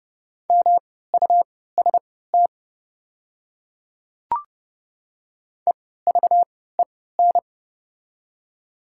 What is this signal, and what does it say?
Morse code, heard as a keyed tone, about 30 wpm, spaced as for 17 wpm, 700 Hz. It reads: MUST EVEN
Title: 3x Character Spacing - Top 300 Words - 30wpm